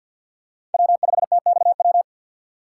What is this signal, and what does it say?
Morse code, heard as a keyed tone, about 47 wpm, 700 Hz. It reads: W5TXW
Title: USA Call Signs - 5 Characters - 2x Character Spacing - 50wpm